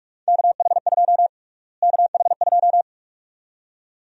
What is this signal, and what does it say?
Morse code, heard as a keyed tone, about 43 wpm, 700 Hz. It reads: KH2 KH2
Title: USA Call Sign Prefixes - Repeated 2x - 45wpm